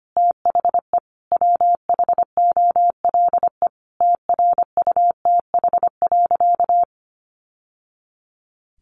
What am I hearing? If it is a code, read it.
Morse code, heard as a keyed tone, about 25 wpm, 700 Hz. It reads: THE WHOLE TRUTH.
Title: Sentences from Top 800 Words 25wpm